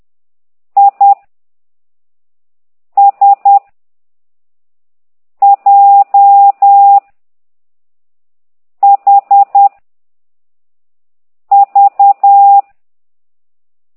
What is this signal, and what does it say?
Morse code, heard as a keyed tone, about 10 wpm, 800 Hz. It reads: I S J H V
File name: exerc7